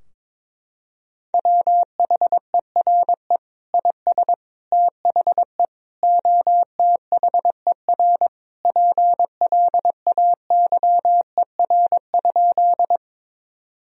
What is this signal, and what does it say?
Morse code, heard as a keyed tone, about 22 wpm, 700 Hz. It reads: WHERE IS THE OTHER PLAYER?
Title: Sentences from Top 1000 Words - Review Part 4 - 22wpm